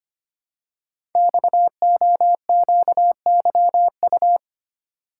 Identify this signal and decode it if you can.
Morse code, heard as a keyed tone, about 25 wpm, 700 Hz. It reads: XOQYU
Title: Code Groups - 5 Letters - 25wpm